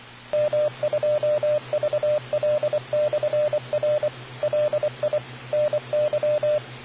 Morse code, heard as a keyed tone, about 24 wpm, 600 Hz, with radio noise. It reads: M2VL/R LI NY